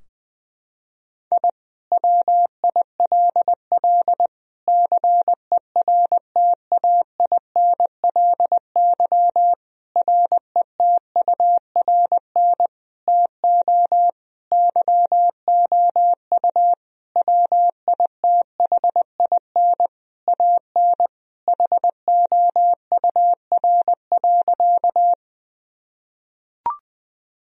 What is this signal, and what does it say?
Morse code, heard as a keyed tone, about 21 wpm, 700 Hz. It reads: I WILL CERTAINLY RETURN TO YOU WITHIN AN HOUR.
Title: Sentences from Top 1000 Words - Review Part 4 - 20wpm